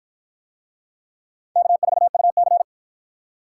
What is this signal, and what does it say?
Morse code, heard as a keyed tone, about 48 wpm, 700 Hz. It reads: K4UC